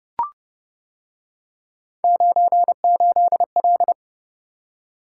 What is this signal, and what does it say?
Morse code, heard as a keyed tone, about 28 wpm, 700 Hz. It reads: 98L